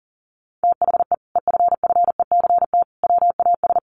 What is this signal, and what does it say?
Morse code, heard as a keyed tone, about 40 wpm, 700 Hz. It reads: THE EFFECT WAS
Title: Sentences from Top 1200 Words 40wpm